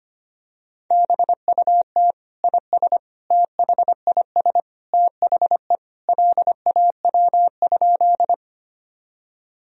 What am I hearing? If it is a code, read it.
Morse code, heard as a keyed tone, about 25 wpm, 700 Hz. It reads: BUT IS THIS THE LAW?
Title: Sentences from Top 1000 Words - Review Part 3 - 25wpm